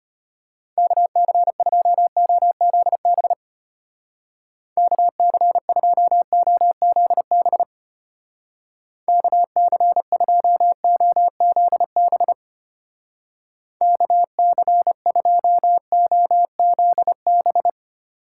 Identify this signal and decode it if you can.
Morse code, heard as a keyed tone, about 32 wpm, 700 Hz. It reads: KC2OZB KC2OZB KC2OZB KC2OZB